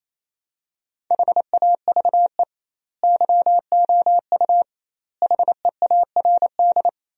Morse code, heard as a keyed tone, about 28 wpm, 700 Hz. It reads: HAVE YOU HEARD
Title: 2x Word-Spacing - Sentences from Top 400 Words - 28wpm